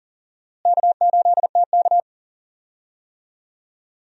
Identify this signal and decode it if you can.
Morse code, heard as a keyed tone, about 40 wpm, 700 Hz. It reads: K8TK
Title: USA Call Signs - 4 Characters - Repeated 3x - 40wpm